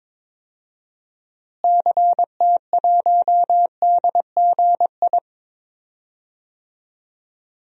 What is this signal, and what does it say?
Morse code, heard as a keyed tone, about 22 wpm, 700 Hz. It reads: CT1DGI
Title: DXCC-100 Call Signs - 6 Characters - Repeated 2x - 22wpm